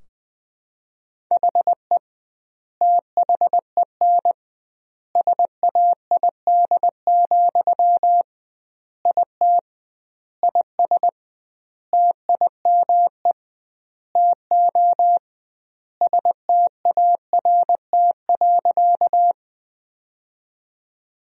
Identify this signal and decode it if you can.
Morse code, heard as a keyed tone, about 20 wpm, 700 Hz. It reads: HE THEN SAID, IT IS TIME TO START.
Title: 2x Word-Spacing - Sentences from Top 300 Words - 20wpm